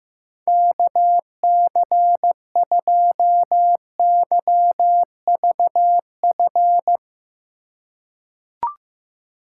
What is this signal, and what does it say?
Morse code, heard as a keyed tone, about 15 wpm, 700 Hz. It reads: KC2YVF
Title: USA Call Signs - 6x Character Spacing - 15wpm